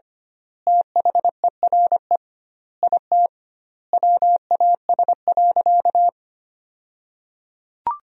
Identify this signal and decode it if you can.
Morse code, heard as a keyed tone, about 25 wpm, 700 Hz. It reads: THERE IT WAS.
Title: 2x Word-Spacing - Sentences from Top 100 Words - 25wpm